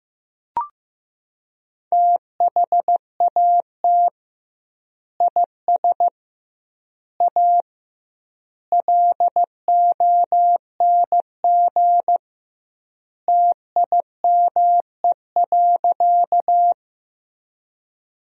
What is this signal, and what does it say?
Morse code, heard as a keyed tone, about 15 wpm, 700 Hz. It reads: THAT IS A LONG TIME.
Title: 2x Word-Spacing - Sentences from Top 100 Words - 15wpm